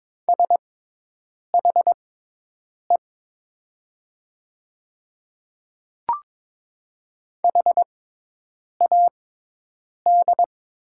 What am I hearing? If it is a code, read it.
Morse code, heard as a keyed tone, about 22 wpm, spaced as for 8 wpm, 700 Hz. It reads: SHE HAD
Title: 6x Character Spacing - Top 100 Words - 22wpm